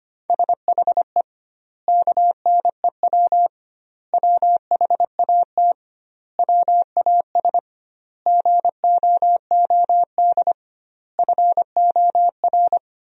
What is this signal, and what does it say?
Morse code, heard as a keyed tone, about 25 wpm, 700 Hz. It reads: SHE KNEW WHAT WAS GOOD FOR
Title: 2x Word-Spacing - Sentences from Top 400 Words - 25wpm